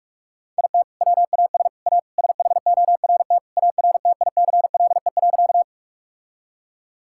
Morse code, heard as a keyed tone, about 46 wpm, 700 Hz. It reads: IT WAS A SHORT ARTICLE.
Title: Sentences from Top 1000 Words - Review Part 5 - 45wpm